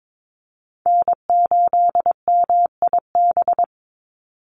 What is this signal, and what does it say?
Morse code, heard as a keyed tone, about 22 wpm, 700 Hz. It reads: N8MIB
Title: USA Call Signs 22wpm